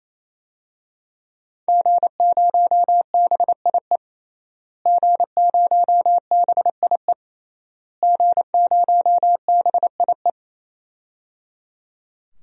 Morse code, heard as a keyed tone, about 28 wpm, 700 Hz. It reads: G0BIE G0BIE G0BIE
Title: DXCC-100 Call Signs - 5 Characters - Repeated 3x - 28wpm